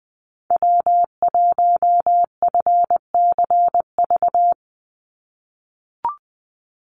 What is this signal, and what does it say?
Morse code, heard as a keyed tone, about 20 wpm, 700 Hz. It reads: W1FCV
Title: Call Signs - 20wpm